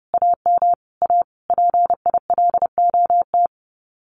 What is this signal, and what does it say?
Morse code, heard as a keyed tone, about 30 wpm, 700 Hz. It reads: AM A PILOT
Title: Common QSO Phrases Encore 30wpm